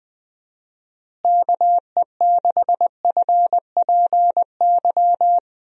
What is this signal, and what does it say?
Morse code, heard as a keyed tone, about 20 wpm, 700 Hz. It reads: KE6FPY
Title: USA Call Signs - 6 Characters - Repeated 3x - 20wpm